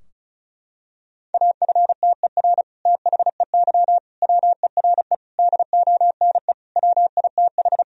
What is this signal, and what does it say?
Morse code, heard as a keyed tone, about 36 wpm, 700 Hz. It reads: AFTER THEY WERE DONE WITH
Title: Sentences from Top 1000 Words - Review Part 2 - 35wpm